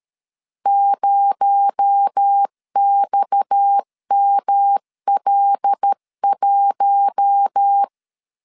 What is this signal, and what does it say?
Morse code, heard as a keyed tone, about 13 wpm, 800 Hz. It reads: 0XML1
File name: CW / DET / 1 6 1500C S C 12wpm 3533